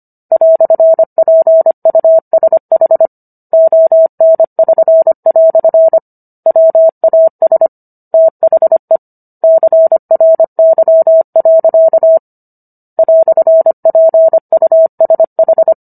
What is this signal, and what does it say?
Morse code, heard as a keyed tone, about 25 wpm, 650 Hz. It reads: "PUSH ON<SN>" WAS THE CRY. "PUSH